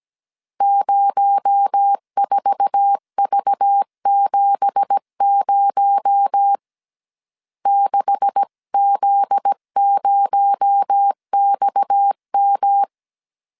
Morse code, heard as a keyed tone, about 17 wpm, 800 Hz. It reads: 04V70 6Z0XM
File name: CW / DET / 1 9 1500C S C 16wpm 2909